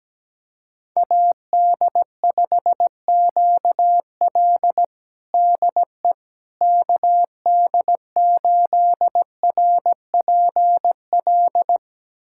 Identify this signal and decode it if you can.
Morse code, heard as a keyed tone, about 17 wpm, 700 Hz. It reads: AD5QL DE KD8RPL
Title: USA Callsign DE Callsign 17wpm